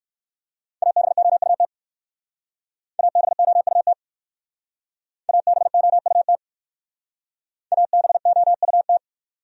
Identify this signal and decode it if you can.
Morse code, heard as a keyed tone, about 57 wpm, 700 Hz. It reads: ABOUT ABOUT ABOUT ABOUT